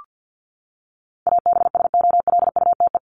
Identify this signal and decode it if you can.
Morse code, heard as a keyed tone, about 50 wpm, 700 Hz. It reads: ABSOLUTE
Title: LY Suffix - Top 100 Words with Base Words 50wpm